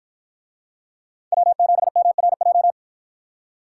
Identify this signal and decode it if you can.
Morse code, heard as a keyed tone, about 52 wpm, 700 Hz. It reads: W7MRJ